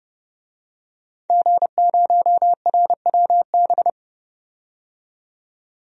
Morse code, heard as a keyed tone, about 29 wpm, 700 Hz. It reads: G0RWB
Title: DXCC-100 Call Signs - 5 Characters - 30wpm